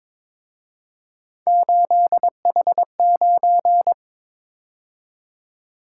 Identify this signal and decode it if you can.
Morse code, heard as a keyed tone, about 27 wpm, 700 Hz. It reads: 8H9
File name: Speed-Racing - Three Letter-Number Combinations - 22wpm